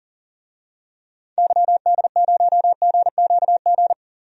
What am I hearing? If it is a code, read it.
Morse code, heard as a keyed tone, about 38 wpm, 700 Hz. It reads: YD0GQG
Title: DXCC-100 Call Signs - 6 Characters - 40wpm